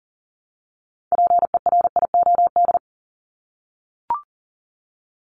Morse code, heard as a keyed tone, about 39 wpm, 700 Hz. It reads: PERIOD
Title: Top 501-750 Words 40wpm